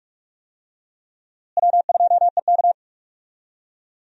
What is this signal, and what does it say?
Morse code, heard as a keyed tone, about 45 wpm, 700 Hz. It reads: W2EK
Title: USA Call Signs - 4 Characters - Repeated 2x - 45wpm